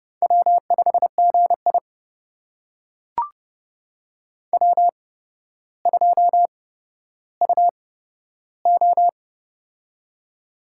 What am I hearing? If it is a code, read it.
Morse code, heard as a keyed tone, about 29 wpm, spaced as for 13 wpm, 700 Hz. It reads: W5GI W2UO
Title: USA Call Signs - 4 Characters - 8x Character Spacing - 30wpm